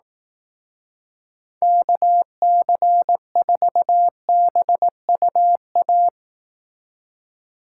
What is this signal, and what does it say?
Morse code, heard as a keyed tone, about 19 wpm, 700 Hz. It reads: KC4BUA
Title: Speed Racing - USA 2x3 Call Signs 15wpm